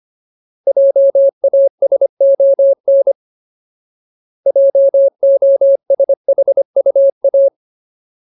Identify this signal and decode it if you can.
Morse code, heard as a keyed tone, about 25 wpm, 550 Hz. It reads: JASON JOSHUA
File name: Ultimate Word List / 25WPM Ultimate